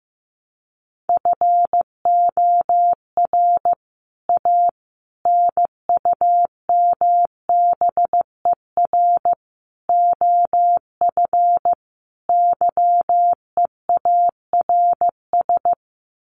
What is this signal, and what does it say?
Morse code, heard as a keyed tone, about 15 wpm, 700 Hz. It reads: FOR A NUMBER OF YEARS
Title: Sets of 5 Words Encore 15wpm